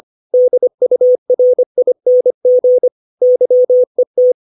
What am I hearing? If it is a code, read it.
Morse code, heard as a keyed tone, about 25 wpm, 500 Hz. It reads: DURING YET
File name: 632 words / 632 Words 25WPM